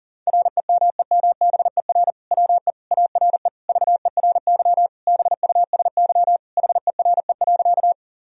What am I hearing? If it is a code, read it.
Morse code, heard as a keyed tone, about 37 wpm, 700 Hz. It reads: REMEMBER WE ARE VERY BUSY HERE.
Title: Sentences from Top 1000 Words - Review Part 3 - 40wpm